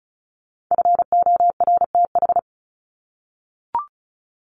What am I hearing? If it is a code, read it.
Morse code, heard as a keyed tone, about 34 wpm, 700 Hz. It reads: FORTH
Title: Top 1751-2000 Words 35wpm